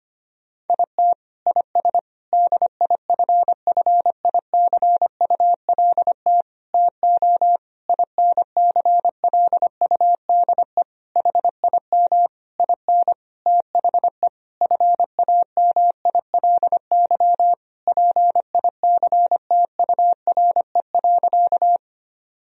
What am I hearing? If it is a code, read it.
Morse code, heard as a keyed tone, about 25 wpm, 700 Hz. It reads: IT IS DIFFICULT TO INCLUDE HIM IN THE FAMILY PICTURE.
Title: Sentences from Top 1000 Words - Review Part 3 - 25wpm